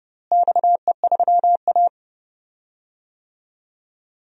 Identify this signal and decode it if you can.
Morse code, heard as a keyed tone, about 31 wpm, 700 Hz. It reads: XE3A
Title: DXCC-100 Call Signs - 4 Characters - Repeated 2x - 30wpm